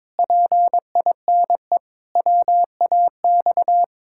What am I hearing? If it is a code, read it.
Morse code, heard as a keyed tone, about 21 wpm, 700 Hz. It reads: PINE WAX
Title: Pangram Challenge v2 - 22wpm